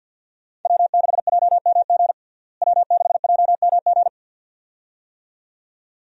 Morse code, heard as a keyed tone, about 48 wpm, 700 Hz. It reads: W6JMG W6JMG
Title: USA Call Signs - 5 Characters - Repeated 2x - 50wpm